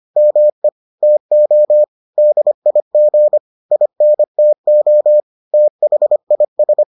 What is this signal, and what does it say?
Morse code, heard as a keyed tone, about 25 wpm, 600 Hz. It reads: ME TO DIG INTO THIS